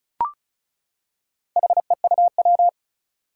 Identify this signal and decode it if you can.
Morse code, heard as a keyed tone, about 36 wpm, 700 Hz. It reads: HEUW
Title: Four Challenging Letter-Number Combinations 35wpm